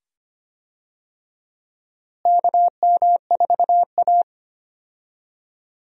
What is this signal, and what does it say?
Morse code, heard as a keyed tone, about 25 wpm, 700 Hz. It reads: KM4A
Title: USA Call Signs - 4 Characters - Repeated 2x - 25wpm